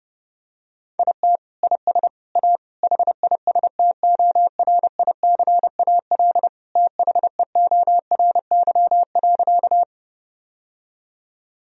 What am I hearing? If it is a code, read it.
Morse code, heard as a keyed tone, about 32 wpm, 700 Hz. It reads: IT IS A HISTORICAL THEORY.
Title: Sentences from Top 1000 Words - Review Part 4 - 30wpm